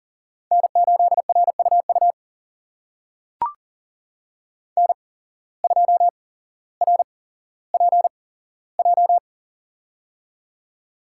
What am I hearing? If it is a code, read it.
Morse code, heard as a keyed tone, about 40 wpm, spaced as for 17 wpm, 700 Hz. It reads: N8RUU N2RPJ
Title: USA Call Signs - 5 Characters - 8x Character Spacing - 40wpm